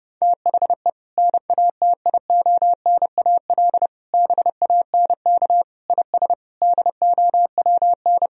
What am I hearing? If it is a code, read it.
Morse code, heard as a keyed tone, about 30 wpm, 700 Hz. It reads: THE NATIONAL BANK IS DOWN
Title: Sentences from Top 1000 Words - Review Part 4 - 30wpm